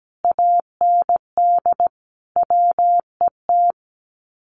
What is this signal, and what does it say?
Morse code, heard as a keyed tone, about 17 wpm, 700 Hz. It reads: AND WET
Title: Common QSO Phrases Encore 17wpm